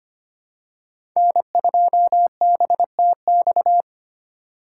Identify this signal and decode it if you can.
Morse code, heard as a keyed tone, about 25 wpm, 700 Hz. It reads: N2BTX
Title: K1USN SST QSO Practice - 25wpm